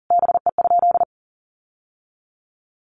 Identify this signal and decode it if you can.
Morse code, heard as a keyed tone, about 40 wpm, 700 Hz. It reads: BE?
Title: Sentences from Top 1200 Words 40wpm